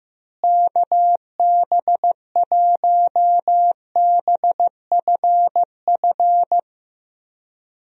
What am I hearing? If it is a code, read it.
Morse code, heard as a keyed tone, about 15 wpm, 700 Hz. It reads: KB1BFF
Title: USA Call Signs - 6 Characters - Repeated 3x - 15wpm